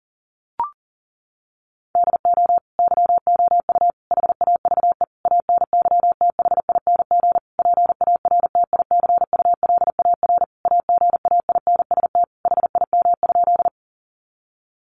Morse code, heard as a keyed tone, about 40 wpm, 700 Hz. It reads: DO YOU HAVE ANYTHING PARTICULAR AGAINST HIM?
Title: Sentences from Top 1000 Words 40wpm